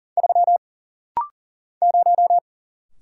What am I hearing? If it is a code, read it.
Morse code, heard as a keyed tone, about 40 wpm, 700 Hz. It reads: 3 0
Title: Single Letter-Number - Rapid-Fire 40wpm